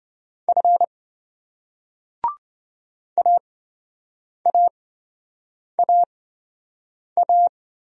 Morse code, heard as a keyed tone, about 27 wpm, spaced as for 7 wpm, 700 Hz. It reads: F AAAA